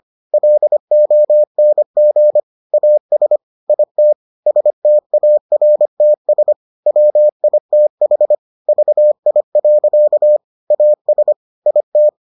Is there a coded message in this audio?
Yes: Morse code from 0.3 to 12.1 s, heard as a keyed tone, about 25 wpm, 600 Hz. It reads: LONG AS IT STARTS WITH VI. AS IT